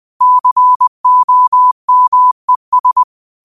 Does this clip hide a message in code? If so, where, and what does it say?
Yes: Morse code from 0.2 to 3.0 s, heard as a keyed tone, about 20 wpm, 1000 Hz. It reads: COMES